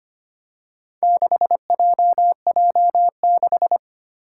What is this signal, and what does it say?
Morse code, heard as a keyed tone, about 25 wpm, 700 Hz. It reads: 6JJ6
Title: Four Challenging Letter-Number Combinations 25wpm